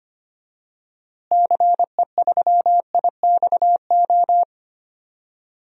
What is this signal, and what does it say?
Morse code, heard as a keyed tone, about 25 wpm, 700 Hz. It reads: CE3IXO
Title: DXCC-100 Call Signs - 6 Characters - Repeated 2x - 25wpm